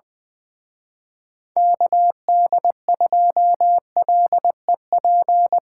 Morse code, heard as a keyed tone, about 23 wpm, 700 Hz. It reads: KD2LEP
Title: Speed-Racing Common QSO Phrases - 20wpm